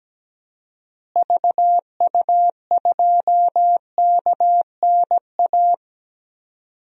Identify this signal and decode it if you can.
Morse code, heard as a keyed tone, about 17 wpm, 700 Hz. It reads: VU2KNA
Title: DXCC-100 Call Signs - 6 Characters - Repeated 2x - 17wpm